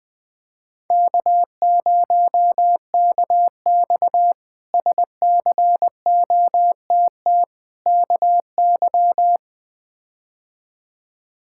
Morse code, heard as a keyed tone, about 20 wpm, 700 Hz. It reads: K0KX SCOTT KY
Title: K1USN SST QSO Practice - 20wpm